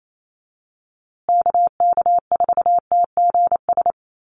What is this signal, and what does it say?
Morse code, heard as a keyed tone, about 28 wpm, 700 Hz. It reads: KK4TGS
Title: USA Call Signs Repeated 3x 28wpm